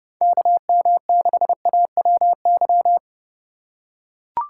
Morse code, heard as a keyed tone, about 30 wpm, 700 Hz. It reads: KM6AWY
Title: USA Call Signs - 6 Characters - 3x Character Spacing - 30wpm